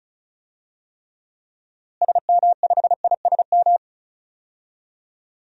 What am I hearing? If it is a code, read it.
Morse code, heard as a keyed tone, about 36 wpm, 700 Hz. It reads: SM5ISM
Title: DXCC-100 Call Signs - 6 Characters - Repeated 3x - 35wpm